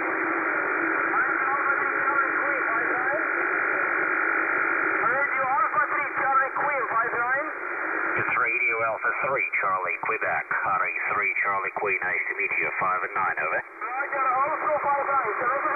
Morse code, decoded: none